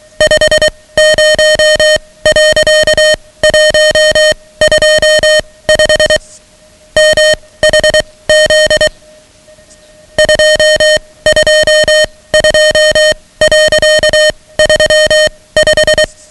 Morse code, heard as a keyed tone, about 23 wpm, spaced as for 17 wpm, 620 Hz, with noise. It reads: 50.125 MHZ 222.35
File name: ham radio / CodeQuick / Code Quick 10